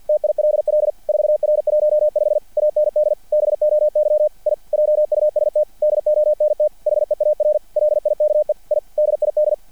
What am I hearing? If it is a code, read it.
Morse code, heard as a keyed tone, about 50 wpm, 600 Hz. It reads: TILL 4R15 AND BOY I JUST DONT HEAR LIKE I DID
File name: archived / CRSnet